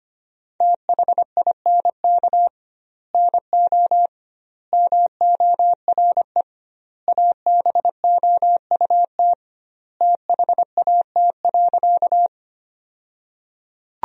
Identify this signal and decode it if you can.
Morse code, heard as a keyed tone, about 25 wpm, 700 Hz. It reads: THINK NO MORE ABOUT THAT.
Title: 2x Word-Spacing - Sentences from Top 200 Words - 25wpm